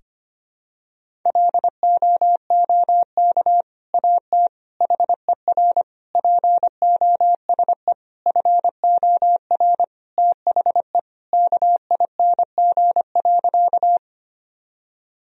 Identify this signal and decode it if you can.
Morse code, heard as a keyed tone, about 25 wpm, 700 Hz. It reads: LOOK AT HER POSE FOR THE KING.
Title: Sentences from Top 1000 Words - Review Part 2 - 25wpm